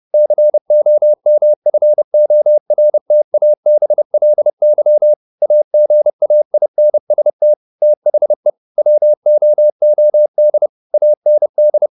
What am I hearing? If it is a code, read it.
Morse code, heard as a keyed tone, about 30 wpm, 600 Hz. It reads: COMFORTABLY AGAINST THE WOOD AND
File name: the old man and the sea / the old man and the sea 1 30wpm